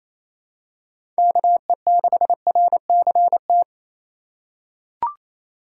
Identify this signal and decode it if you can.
Morse code, heard as a keyed tone, about 28 wpm, 700 Hz. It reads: KE6RCT
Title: USA Call Signs - 3x Character Spacing - 28wpm